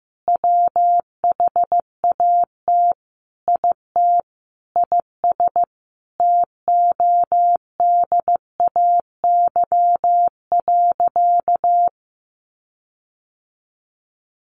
Morse code, heard as a keyed tone, about 14 wpm, 700 Hz. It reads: WHAT IT IS TODAY.
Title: Sentences from Top 1500 Words 15wpm